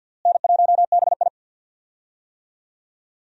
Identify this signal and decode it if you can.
Morse code, heard as a keyed tone, about 47 wpm, 700 Hz. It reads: N1BI
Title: USA Call Signs - 4 Characters - Repeated 2x - 50wpm